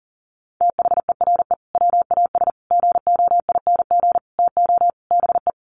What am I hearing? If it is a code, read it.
Morse code, heard as a keyed tone, about 40 wpm, 700 Hz. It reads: THERE WAS GOING TO BE
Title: Sets of 5 Words Encore2 40wpm